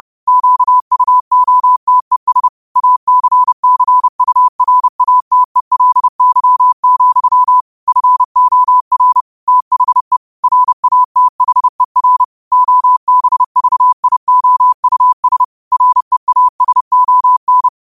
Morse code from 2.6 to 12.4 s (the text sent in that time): ACCURATELY, FOR THE RATHER